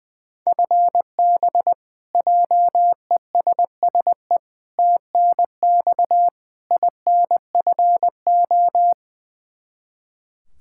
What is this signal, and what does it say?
Morse code, heard as a keyed tone, about 21 wpm, 700 Hz. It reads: FB JESSE TNX INFO
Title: K1USN SST QSO Practice 20wpm